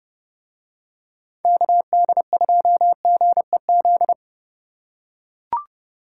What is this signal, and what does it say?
Morse code, heard as a keyed tone, about 29 wpm, 700 Hz. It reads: KD2GEZ